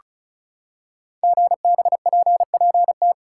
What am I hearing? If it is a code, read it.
Morse code, heard as a keyed tone, about 36 wpm, 690 Hz. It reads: GBPPT
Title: Code Groups - 5 Letters - 35wpm